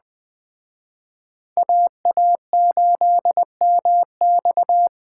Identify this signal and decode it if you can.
Morse code, heard as a keyed tone, about 20 wpm, 700 Hz. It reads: AA8MX
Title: Speed Racing - USA 2x2 Call Signs 17wpm